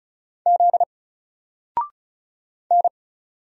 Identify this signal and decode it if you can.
Morse code, heard as a keyed tone, about 34 wpm, 700 Hz. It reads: Z N
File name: Single-Letter-Number-35wpm